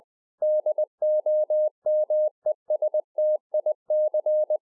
Morse code, heard as a keyed tone, about 20 wpm, 610 Hz. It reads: DOMESTIC